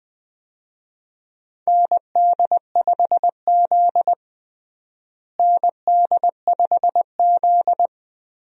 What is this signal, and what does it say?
Morse code, heard as a keyed tone, about 20 wpm, 700 Hz. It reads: ND5Z ND5Z
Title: USA Call Signs - 4 Characters - Repeated 2x - 20wpm